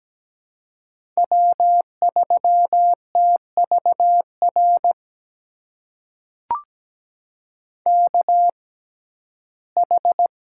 Morse code, heard as a keyed tone, about 17 wpm, spaced as for 12 wpm, 700 Hz. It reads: W3TVR KH
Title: USA Call Signs - 5 Characters - 6x Character Spacing - 17wpm